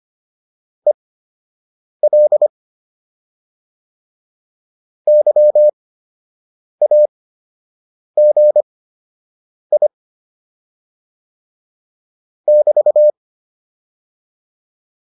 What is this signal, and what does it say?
Morse code, heard as a keyed tone, about 25 wpm, spaced as for 7 wpm, 600 Hz. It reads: EL YAGI <BT>